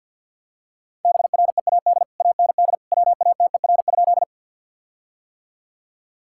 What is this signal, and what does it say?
Morse code, heard as a keyed tone, about 50 wpm, 700 Hz. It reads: BREAD AND WATER?
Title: Sentences from Top 1600 Words 50wpm